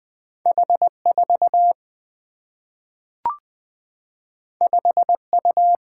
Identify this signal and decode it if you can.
Morse code, heard as a keyed tone, about 20 wpm, 700 Hz. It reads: H 4 5 U
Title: Single-Letter Single-Number 20wpm